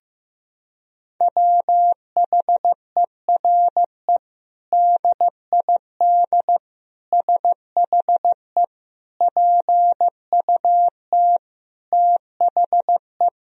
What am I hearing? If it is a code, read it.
Morse code, heard as a keyed tone, about 15 wpm, 700 Hz. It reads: WHERE DID SHE PUT THE
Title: Sentences from Top 1000 Words - Review Part 3 - 15wpm